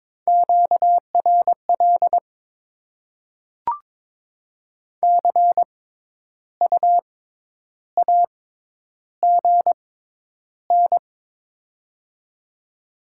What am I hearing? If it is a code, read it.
Morse code, heard as a keyed tone, about 22 wpm, spaced as for 10 wpm, 700 Hz. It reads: QRL CU AGN
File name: Top 20 QSO Elements - 6x Character Spacing - 22wpm